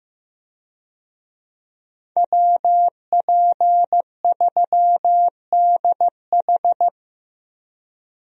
Morse code, heard as a keyed tone, about 15 wpm, 700 Hz. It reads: WP3DH
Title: USA Call Signs - 5 Characters - Repeated 2x - 15wpm